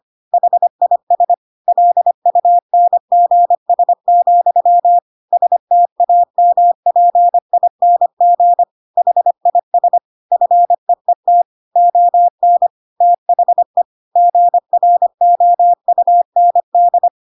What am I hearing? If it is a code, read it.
Morse code, heard as a keyed tone, about 25 wpm, 700 Hz. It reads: HIS LUNGS, STAMPING HIS FEET ON THE GROUND
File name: Pinocchio / Pinocchio ch5 25 25 ik1hge